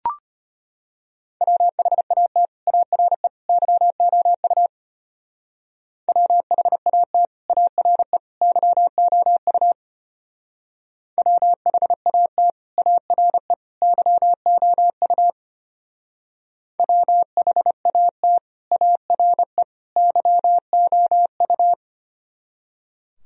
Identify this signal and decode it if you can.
Morse code, heard as a keyed tone, about 30 wpm, 700 Hz. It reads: WHAT ARE YOU WHAT ARE YOU WHAT ARE YOU WHAT ARE YOU
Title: Speed-Racing - Sets of 3 Words 25wpm